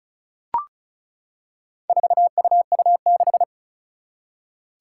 Morse code, heard as a keyed tone, about 36 wpm, 700 Hz. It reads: 4UU6
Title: Four Challenging Letter-Number Combinations 35wpm